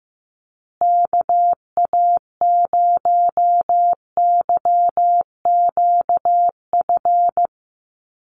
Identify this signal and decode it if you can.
Morse code, heard as a keyed tone, about 15 wpm, 700 Hz. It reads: KA0YQF
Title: USA Call Signs Repeated 2x 15wpm